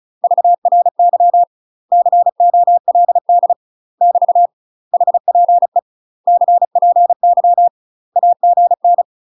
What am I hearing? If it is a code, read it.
Morse code, heard as a keyed tone, about 35 wpm, 700 Hz. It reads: VRY COLD <BT> HPE CPY AGN
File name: UK MASTER QSOs TRAINING / 35WPM / UK MASTER QSOs 0031